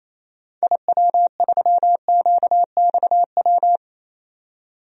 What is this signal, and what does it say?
Morse code, heard as a keyed tone, about 28 wpm, 700 Hz. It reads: IW3QXW
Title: DXCC-100 Call Signs - 6 Characters - 28wpm